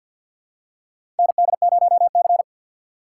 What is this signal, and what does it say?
Morse code, heard as a keyed tone, about 51 wpm, 700 Hz. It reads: ND0C
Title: USA Call Signs - 4 Characters - 2x Character Spacing - 50wpm